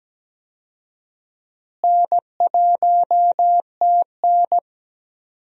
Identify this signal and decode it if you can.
Morse code, heard as a keyed tone, about 17 wpm, 700 Hz. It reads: N1TN
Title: USA Call Signs - 4 Characters - Repeated 2x - 17wpm